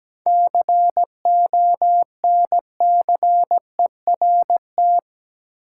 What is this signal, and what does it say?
Morse code, heard as a keyed tone, about 17 wpm, 700 Hz. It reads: CONCERT
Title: Seven Letter Words - 17wpm